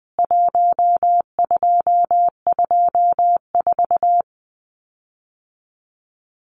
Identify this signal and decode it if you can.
Morse code, heard as a keyed tone, about 20 wpm, 700 Hz. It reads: 1224
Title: CWT Exchange 20wpm